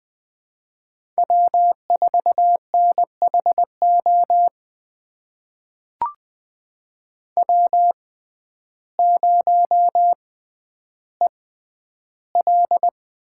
Morse code, heard as a keyed tone, about 20 wpm, spaced as for 11 wpm, 700 Hz. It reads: W4NHO W0EL
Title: USA Call Signs - 5 Characters - 6x Character Spacing - 20wpm